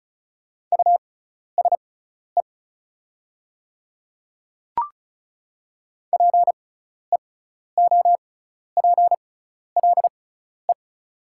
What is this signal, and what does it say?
Morse code, heard as a keyed tone, about 35 wpm, spaced as for 12 wpm, 690 Hz. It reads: USE PEOPLE
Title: 6x Character Spacing - Top 300 Words - 35wpm